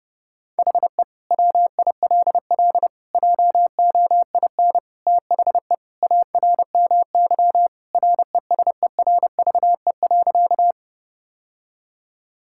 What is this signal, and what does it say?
Morse code, heard as a keyed tone, about 32 wpm, 700 Hz. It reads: HE WILL JOIN THE ARMY RESERVE.
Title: Sentences from Top 1000 Words - Review Part 6 - 30wpm